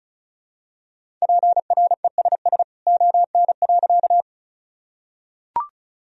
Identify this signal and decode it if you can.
Morse code, heard as a keyed tone, about 36 wpm, 700 Hz. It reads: PRESS ON.
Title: Sentences from Top 1000 Words - Review Part 1 - 35wpm